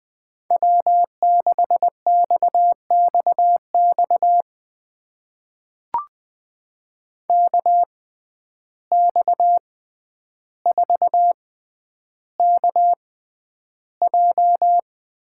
Morse code, heard as a keyed tone, about 20 wpm, spaced as for 10 wpm, 700 Hz. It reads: W6XXX KX4KJ
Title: USA Call Signs - 5 Characters - 6x Character Spacing - 20wpm